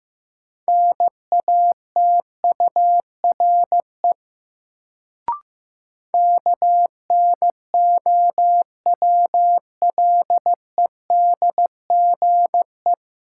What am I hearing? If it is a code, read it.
Morse code, heard as a keyed tone, about 15 wpm, 700 Hz. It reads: NATURE KNOWLEDGE
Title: Sentences from Top 600 Words - New Word Drill - 15wpm